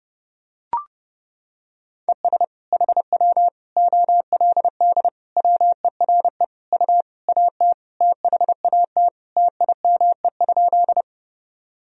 Morse code, heard as a keyed tone, about 30 wpm, 700 Hz. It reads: ES HW OLD WERE U AT THAT TIME?
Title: Sentences from Top 200 Words and Abbreviations 30wpm